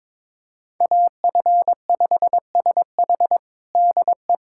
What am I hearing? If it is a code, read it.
Morse code, heard as a keyed tone, about 22 wpm, 700 Hz. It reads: AF5SH DE
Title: USA Callsign DE Callsign 22wpm